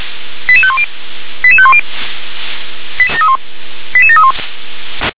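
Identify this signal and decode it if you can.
Five-tone selective call (ZVEI), heard as quick runs of five tones, tones of about 70 ms. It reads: 88510 88510 88731
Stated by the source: ZVEI five-tone selective call received by SDR.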